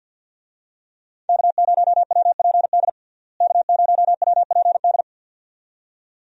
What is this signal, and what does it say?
Morse code, heard as a keyed tone, about 48 wpm, 700 Hz. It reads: K0WPD K0WPD
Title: USA Call Signs - 5 Characters - Repeated 2x - 50wpm